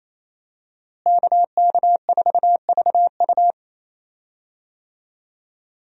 Morse code, heard as a keyed tone, about 28 wpm, 700 Hz. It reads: KK4VU
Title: USA Call Signs - 5 Characters - Repeated 2x - 28wpm